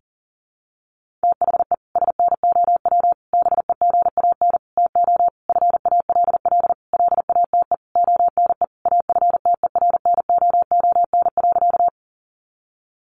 Morse code, heard as a keyed tone, about 40 wpm, 700 Hz. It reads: THE SNOW BEGAN TO FALL LATE ONE AFTERNOON.
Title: Sentences from Top 1500 Words 40wpm